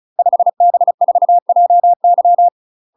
Morse code, heard as a keyed tone, about 34 wpm, 700 Hz. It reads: 5B4JY
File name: UK MASTER QSOs TRAINING/35WPM